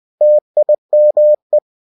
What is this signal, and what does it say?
Morse code, heard as a keyed tone, about 20 wpm, 600 Hz. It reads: TIME
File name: cw-books / winnie-the-pooh / winnie-20wpm-page-0107